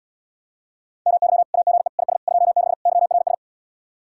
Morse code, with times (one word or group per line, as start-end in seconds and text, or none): 1.1-1.4 MY
1.5-1.9 AGE
2.0-2.2 IS
2.3-2.7 26
2.9-3.3 YRS